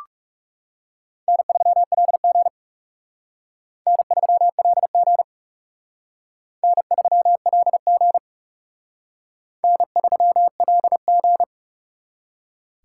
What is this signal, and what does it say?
Morse code, heard as a keyed tone, about 36 wpm, 700 Hz. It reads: N3LG N3LG N3LG N3LG
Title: Speed Racing - USA 1x2 Call Signs 30wpm